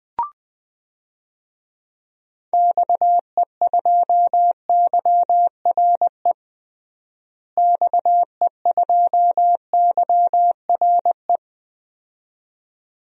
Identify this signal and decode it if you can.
Morse code, heard as a keyed tone, about 20 wpm, 700 Hz. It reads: XE2YRE XE2YRE
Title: DXCC-100 Call Signs - 6 Characters - Repeated 2x - 20wpm